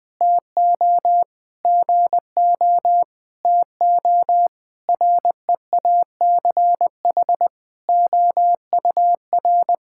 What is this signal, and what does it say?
Morse code, heard as a keyed tone, about 20 wpm, 700 Hz. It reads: TO GO TO REACH OUR
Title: Sentences with 7-Word Phrases - 20wpm